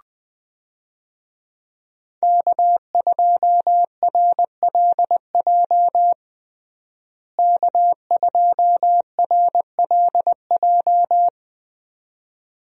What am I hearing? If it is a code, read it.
Morse code, heard as a keyed tone, about 20 wpm, 700 Hz. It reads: K2RLJ K2RLJ
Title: USA Call Signs - 5 Characters - Repeated 2x - 20wpm